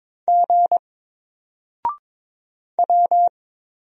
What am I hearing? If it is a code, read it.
Morse code, heard as a keyed tone, about 22 wpm, 700 Hz. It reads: G W
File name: Single-Letter-Number-22wpm